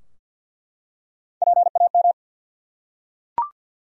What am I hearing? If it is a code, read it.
Morse code, heard as a keyed tone, about 51 wpm, 700 Hz. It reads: PAM
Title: USA Common Names - Max 3 Letters - 50wpm